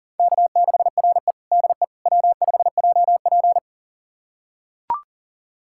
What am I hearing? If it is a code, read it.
Morse code, heard as a keyed tone, about 40 wpm, 700 Hz. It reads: K6RE DE W5JP
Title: USA Callsign DE Callsign - 4 Characters - 40wpm